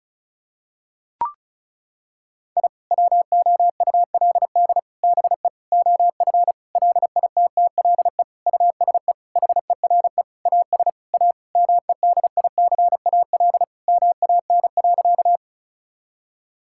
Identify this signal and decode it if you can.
Morse code, heard as a keyed tone, about 36 wpm, 700 Hz. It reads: I WOULD BE OF LITTLE USE HERE AS A MEDICAL MAN.
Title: Sentences from Top 1000 Words - Review Part 5 - 35wpm